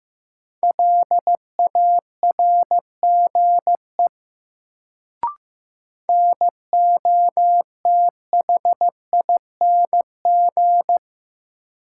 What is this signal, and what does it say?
Morse code, heard as a keyed tone, about 15 wpm, 700 Hz. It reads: LARGE NOTHING
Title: Sets of 4 Words - New Word Drill - 15wpm